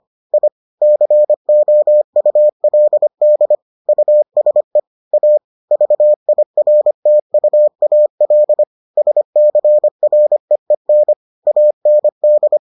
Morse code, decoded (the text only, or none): I COULD USE A VIRTUAL SCREEN AND